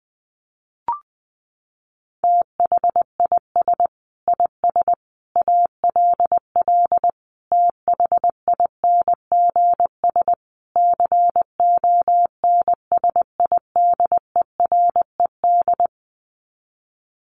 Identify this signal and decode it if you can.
Morse code, heard as a keyed tone, about 20 wpm, 700 Hz. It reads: THIS IS ALL THINGS CONSIDERED
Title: Sets of 5 Words 20wpm